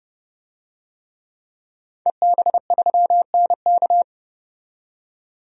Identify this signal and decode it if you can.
Morse code, heard as a keyed tone, about 30 wpm, 700 Hz. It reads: EB3NK
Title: DXCC-100 Call Signs - 5 Characters - 30wpm